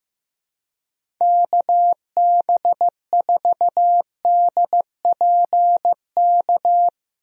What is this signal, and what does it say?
Morse code, heard as a keyed tone, about 15 wpm, 700 Hz. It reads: KB4DPK
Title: USA Call Signs - 6 Characters - 15wpm